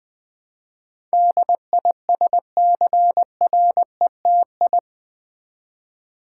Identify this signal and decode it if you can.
Morse code, heard as a keyed tone, about 20 wpm, 700 Hz. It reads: DISCRETI
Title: Build A Word - 4 Letters At A Time - 10-Letter Words - 20wpm